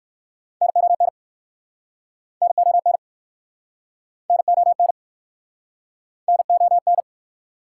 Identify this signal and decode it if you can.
Morse code, heard as a keyed tone, about 61 wpm, 690 Hz. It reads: NON NON NON NON